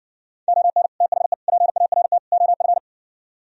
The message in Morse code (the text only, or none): ON THE PART OF